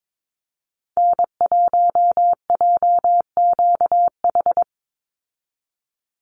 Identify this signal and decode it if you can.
Morse code, heard as a keyed tone, about 22 wpm, 700 Hz. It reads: N1JQH
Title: USA Call Signs 22wpm